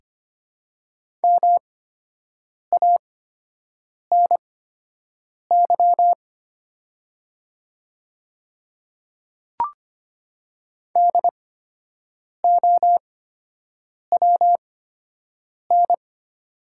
Morse code, heard as a keyed tone, about 25 wpm, spaced as for 7 wpm, 700 Hz. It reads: MANY DOWN
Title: 8x Character Spacing - Top 100 Words - 25wpm